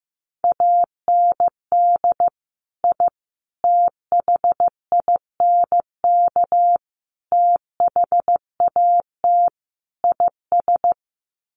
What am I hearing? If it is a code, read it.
Morse code, heard as a keyed tone, about 15 wpm, 700 Hz. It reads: AND I THINK THAT IS
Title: Sets of 5 Words Encore2 15wpm